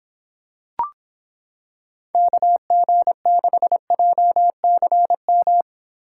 Morse code, heard as a keyed tone, about 26 wpm, 700 Hz. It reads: KG6JCM